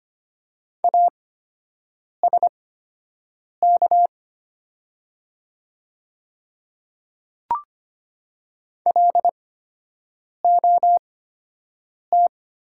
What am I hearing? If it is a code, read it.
Morse code, heard as a keyed tone, about 25 wpm, spaced as for 7 wpm, 700 Hz. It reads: ASK LOT
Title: 8x Character Spacing - Top 500 Words - 25wpm